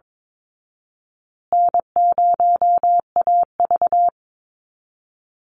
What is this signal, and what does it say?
Morse code, heard as a keyed tone, about 22 wpm, 700 Hz. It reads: N0AV
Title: Easy Call Signs Repeated 2x - 22wpm